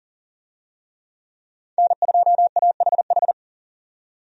Morse code, heard as a keyed tone, about 40 wpm, 700 Hz. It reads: N2AHH
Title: USA Call Signs - 5 Characters - 40wpm